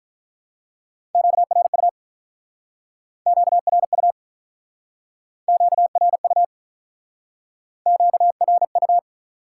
Morse code, heard as a keyed tone, about 47 wpm, 700 Hz. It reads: QRU QRU QRU QRU